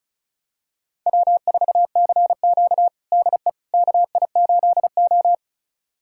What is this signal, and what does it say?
Morse code, heard as a keyed tone, about 36 wpm, 700 Hz. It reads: W4CQ DE KI8O
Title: USA Callsign DE Callsign - 4 Characters - 35wpm